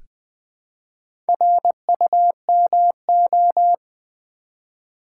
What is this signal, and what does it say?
Morse code, heard as a keyed tone, about 20 wpm, 700 Hz. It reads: RUMO
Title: Build A Word - 1 Letter At A Time - 5-Letter Words - 20wpm